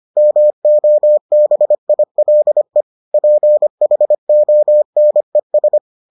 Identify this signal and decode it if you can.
Morse code, heard as a keyed tone, about 25 wpm, 600 Hz. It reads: MOBILE PHONES